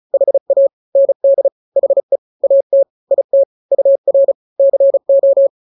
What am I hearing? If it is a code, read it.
Morse code, heard as a keyed tone, about 34 wpm, 550 Hz. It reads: HA ND HE AT IT UR CO
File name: n-grams / 2-grams 35 30 WPM